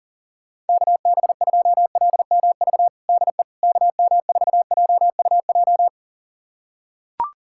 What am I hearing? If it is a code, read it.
Morse code, heard as a keyed tone, about 37 wpm, 700 Hz. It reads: KB2LMV DE KM4JUJ
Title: USA Callsign DE Callsign - 6 Characters - 40wpm